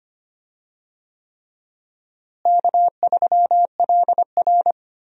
Morse code, heard as a keyed tone, about 25 wpm, 700 Hz. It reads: K3LR
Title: USA Call Signs - 4 Characters - Repeated 3x - 25wpm